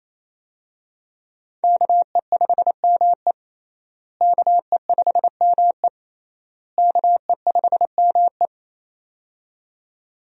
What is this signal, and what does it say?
Morse code, heard as a keyed tone, about 28 wpm, 700 Hz. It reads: KE5ME KE5ME KE5ME
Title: USA Call Signs - 5 Characters - Repeated 3x - 28wpm